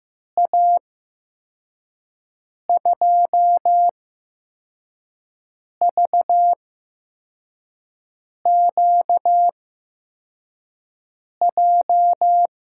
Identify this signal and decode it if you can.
Morse code, heard as a keyed tone, about 15 wpm, spaced as for 4 wpm, 700 Hz. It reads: A2VQJ
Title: USA Call Signs - 6 Characters - 8x Character Spacing - 15wpm